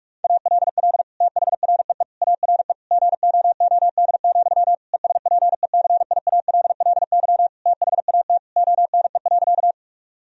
Morse code, heard as a keyed tone, about 46 wpm, 700 Hz. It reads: ALL THREE ARE GOOD, ESPECIALLY THAT ONE.
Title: Sentences from Top 1000 Words - Review Part 4 - 45wpm